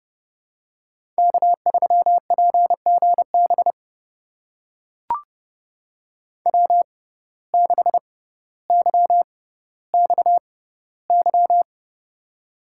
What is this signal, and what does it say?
Morse code, heard as a keyed tone, about 29 wpm, spaced as for 15 wpm, 700 Hz. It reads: K3PGB W6YXY
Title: USA Call Signs - 5 Characters - 6x Character Spacing - 30wpm